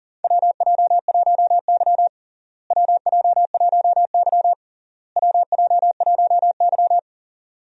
Morse code, heard as a keyed tone, about 40 wpm, 700 Hz. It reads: WJ1Y WJ1Y WJ1Y
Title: USA Call Signs - 4 Characters - Repeated 3x - 40wpm